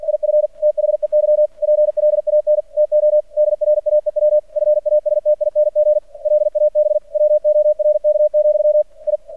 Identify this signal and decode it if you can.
Morse code, heard as a keyed tone, about 52 wpm, 600 Hz. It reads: IF THEY PLAN TO DRAEW FASTING LAB WORK, I
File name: archived / CRSnet